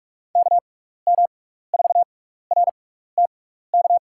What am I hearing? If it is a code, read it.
Morse code, heard as a keyed tone, about 43 wpm, spaced as for 15 wpm, 700 Hz. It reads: KM4RTK
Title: USA Call Signs - 6 Characters - 6x Character Spacing - 45wpm